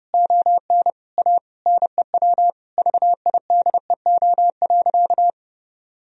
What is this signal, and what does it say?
Morse code, heard as a keyed tone, about 30 wpm, 700 Hz. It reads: ON A NEW VIDEO.
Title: Sentences from Top 1000 Words - Review Part 4 - 30wpm